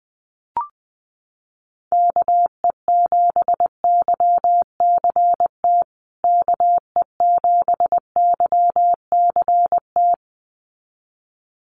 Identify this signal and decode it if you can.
Morse code, heard as a keyed tone, about 20 wpm, 700 Hz. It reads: KE7YCT KE7YCT
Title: USA Call Signs Repeated 2x 20wpm